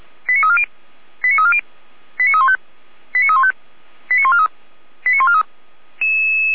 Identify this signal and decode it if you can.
Five-tone selective call (ZVEI), heard as quick runs of five tones, tones of about 70 ms. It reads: 89340 89340 89315 89315 89143 89143